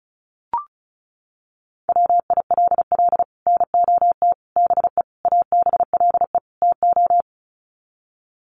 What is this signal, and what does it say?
Morse code, heard as a keyed tone, about 34 wpm, 700 Hz. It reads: WILL NOT BE ABLE TO
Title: Sets of 5 Words 35wpm